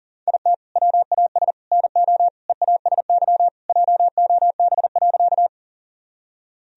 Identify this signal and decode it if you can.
Morse code, heard as a keyed tone, about 40 wpm, 700 Hz. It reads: IT WAS NO EASY JOB.
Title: Sentences from Top 1000 Words - Review Part 3 - 40wpm